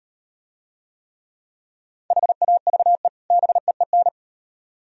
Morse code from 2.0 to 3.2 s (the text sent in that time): HAVE